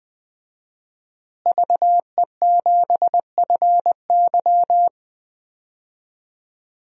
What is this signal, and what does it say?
Morse code, heard as a keyed tone, about 21 wpm, 700 Hz. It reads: VE7FY
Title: DXCC-100 Call Signs - 5 Characters - Repeated 3x - 20wpm